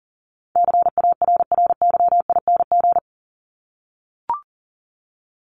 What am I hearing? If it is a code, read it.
Morse code, heard as a keyed tone, about 37 wpm, 700 Hz. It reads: CARRYING
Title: ING Suffix - Top 100 Words with Base Words 40wpm